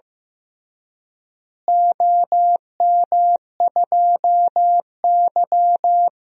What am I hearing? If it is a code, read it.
Morse code, heard as a keyed tone, about 15 wpm, 700 Hz. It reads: OM2Y
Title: DXCC-100 Call Signs - 4 Characters - Repeated 2x - 15wpm